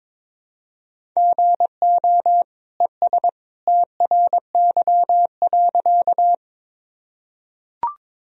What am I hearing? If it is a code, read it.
Morse code, heard as a keyed tone, about 22 wpm, 700 Hz. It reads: GO ES TRY.
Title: Sentences from Top 200 Words and Abbreviations 22wpm